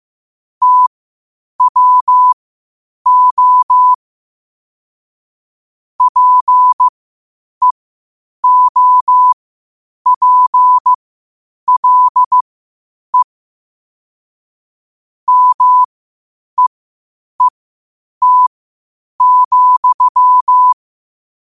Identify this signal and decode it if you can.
Morse code, heard as a keyed tone, about 15 wpm, spaced as for 8 wpm, 1000 Hz. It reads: TWO PEOPLE MEET,
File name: qotdfiles / 8wpm